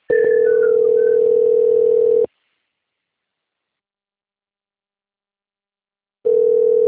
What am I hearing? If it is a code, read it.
Ringback tone (North America), 440 + 480 Hz.